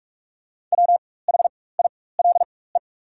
Morse code, heard as a keyed tone, about 46 wpm, spaced as for 21 wpm, 700 Hz. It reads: WHILE
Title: 4x Character Spacing - Top 300 Words - 45wpm